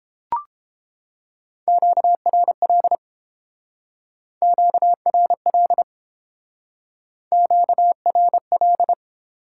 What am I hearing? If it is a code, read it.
Morse code, heard as a keyed tone, about 28 wpm, 700 Hz. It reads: QRL QRL QRL